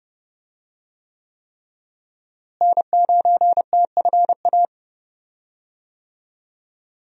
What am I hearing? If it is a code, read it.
Morse code, heard as a keyed tone, about 31 wpm, 700 Hz. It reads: N9TFA
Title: USA Call Signs - 5 Characters - Repeated 3x - 30wpm